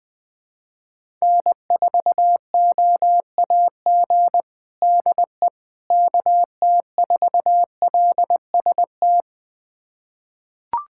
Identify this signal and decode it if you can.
Morse code, heard as a keyed tone, about 20 wpm, 700 Hz. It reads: N4OAG DE KT4LST
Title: USA Callsign DE Callsign 20wpm